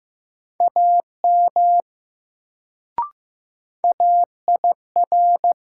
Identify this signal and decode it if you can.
Morse code, heard as a keyed tone, about 15 wpm, 700 Hz. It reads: AM AIR
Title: Sets of 2 Words - New Word Drill - 15wpm